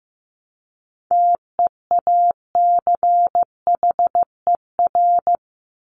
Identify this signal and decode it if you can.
Morse code, heard as a keyed tone, about 15 wpm, 700 Hz. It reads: TEACHER
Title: Top 751-1000 Words 15wpm